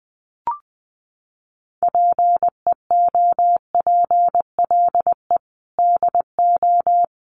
Morse code, heard as a keyed tone, about 20 wpm, 700 Hz. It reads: PEOPLE DO
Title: Sentences from Top 1200 Words 20wpm